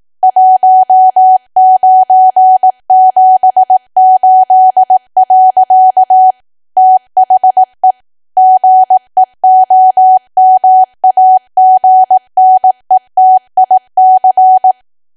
Morse code, heard as a keyed tone, about 18 wpm, 750 Hz. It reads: 1978. THE GEOMAGNETIC
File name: cwa / W1AW-Level2 / 18WPM